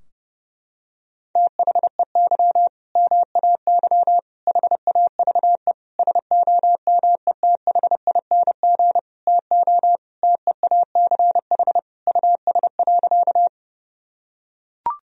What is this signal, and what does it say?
Morse code, heard as a keyed tone, about 32 wpm, 700 Hz. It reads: THEY MAY HAVE SOMETHING TO TEACH US.
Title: Sentences from Top 1000 Words - Review Part 2 - 30wpm